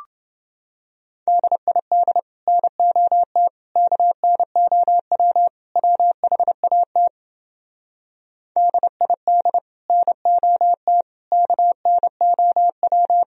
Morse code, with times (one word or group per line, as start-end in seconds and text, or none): 1.3-2.2 DID
2.5-3.5 NOT
3.8-5.5 KNOW
5.8-7.1 WHAT
8.6-9.6 DID
9.9-11.0 NOT
11.3-13.2 KNOW